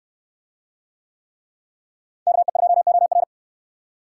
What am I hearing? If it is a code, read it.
Morse code, heard as a keyed tone, about 67 wpm, 700 Hz. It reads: KE2OR